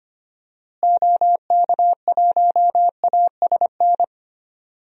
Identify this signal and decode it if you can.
Morse code, heard as a keyed tone, about 25 wpm, 700 Hz. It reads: OK1ASN
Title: DXCC-100 Call Signs - 6 Characters - Repeated 3x - 25wpm